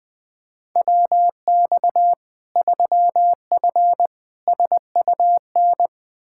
Morse code, heard as a keyed tone, about 20 wpm, 700 Hz. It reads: WX 3F SUN
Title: Practice QSO - Exchange 2 - Info - 20wpm